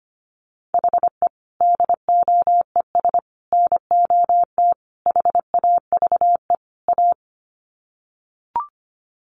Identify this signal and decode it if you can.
Morse code, heard as a keyed tone, about 25 wpm, 700 Hz. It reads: HE DOES NOT HAVE A
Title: Sets of 5 Words Encore 25wpm